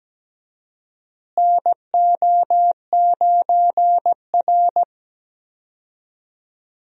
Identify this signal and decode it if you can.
Morse code, heard as a keyed tone, about 17 wpm, 700 Hz. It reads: NO9R
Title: USA Call Signs - 4 Characters - Repeated 2x - 17wpm